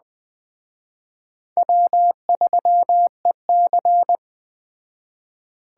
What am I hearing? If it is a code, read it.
Morse code, heard as a keyed tone, about 20 wpm, 700 Hz. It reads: W3EC